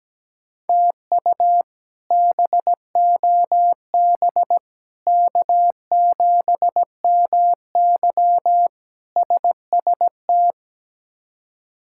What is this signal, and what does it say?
Morse code, heard as a keyed tone, about 17 wpm, 700 Hz. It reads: TU BOB K7MY SST
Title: K1USN SST QSO Practice - 17wpm